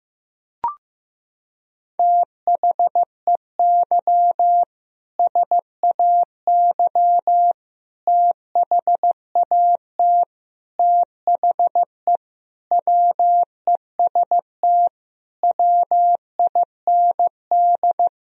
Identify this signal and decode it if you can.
Morse code, heard as a keyed tone, about 16 wpm, 700 Hz. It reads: THEY SAY THAT THE WEST WIND
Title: Sentences from Top 1000 Words - Review Part 3 - 15wpm